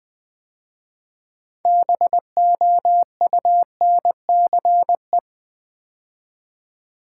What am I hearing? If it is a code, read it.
Morse code, heard as a keyed tone, about 20 wpm, 700 Hz. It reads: BOUNCE
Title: 6 Letter Words - First 3 Letters Given - 20wpm